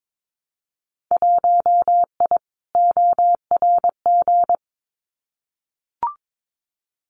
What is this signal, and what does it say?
Morse code, heard as a keyed tone, about 22 wpm, 700 Hz. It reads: 1I ORG
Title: Winter Field Day Exchanges Encore 22wpm